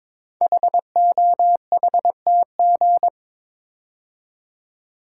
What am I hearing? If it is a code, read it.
Morse code, heard as a keyed tone, about 21 wpm, 700 Hz. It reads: HOHTG
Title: Code Groups - 5 Letters - 22wpm